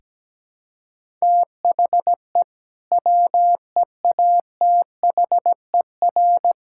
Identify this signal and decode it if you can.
Morse code, heard as a keyed tone, about 17 wpm, 700 Hz. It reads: THE WEATHER
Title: Sentences from Top 1000 Words - Review Part 5 - 17wpm